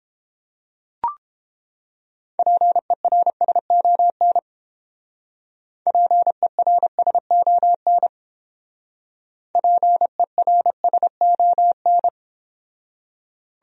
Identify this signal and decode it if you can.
Morse code, heard as a keyed tone, about 30 wpm, 700 Hz. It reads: PERSON PERSON PERSON